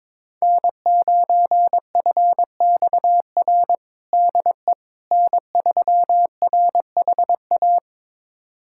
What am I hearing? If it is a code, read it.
Morse code, heard as a keyed tone, about 22 wpm, 700 Hz. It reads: N9FXR DE N3RHA
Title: USA Callsign DE Callsign - 5 Characters - 22wpm